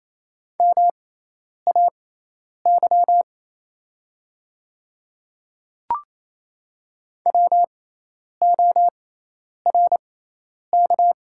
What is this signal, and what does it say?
Morse code, heard as a keyed tone, about 28 wpm, spaced as for 10 wpm, 700 Hz. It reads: MAY WORK
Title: 6x Character Spacing - Top 200 Words - 28wpm